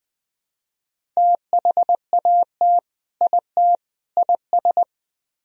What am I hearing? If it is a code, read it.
Morse code, heard as a keyed tone, about 20 wpm, 700 Hz. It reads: THAT IT IS
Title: Speed-Racing - Sets of 3 Words 17wpm